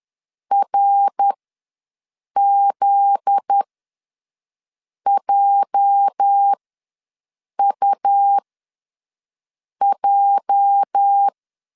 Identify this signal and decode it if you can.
Morse code, heard as a keyed tone, about 10 wpm, spaced as for 5 wpm, 800 Hz. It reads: RZJUJ
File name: LEZIONI / 8 YJ45 / 8 YJ ws 10wpm 4043